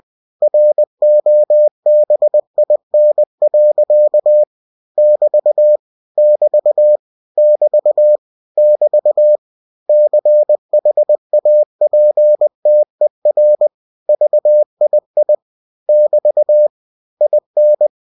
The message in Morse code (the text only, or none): ROBIN. <BT> <BT> <BT> <BT> CHAPTER VII <BT> IN